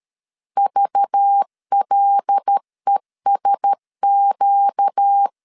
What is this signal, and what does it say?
Morse code, heard as a keyed tone, about 13 wpm, 800 Hz. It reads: VLESQ
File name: CW / DET / 1 6 1500C S A 12wpm 3440